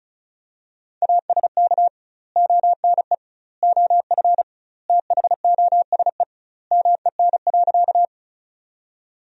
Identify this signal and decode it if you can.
Morse code, heard as a keyed tone, about 34 wpm, 700 Hz. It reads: ASK ONE OF THOSE MEN.
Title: 2x Word-Spacing - Sentences from Top 300 Words - 35wpm